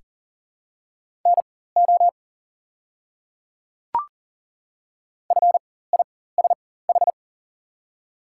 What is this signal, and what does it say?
Morse code, heard as a keyed tone, about 40 wpm, spaced as for 19 wpm, 700 Hz. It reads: NO FISH
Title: 4x Character Spacing - Top 300 Words - 40wpm